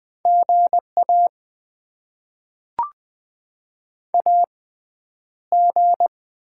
Morse code, heard as a keyed tone, about 20 wpm, 700 Hz. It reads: GA A G
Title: Top 20 QSO Elements - 6x Character Spacing - 20wpm